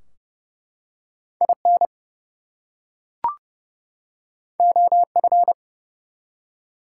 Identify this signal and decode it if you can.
Morse code, heard as a keyed tone, about 30 wpm, 700 Hz. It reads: IN OF
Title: Two Letter Words - 30wpm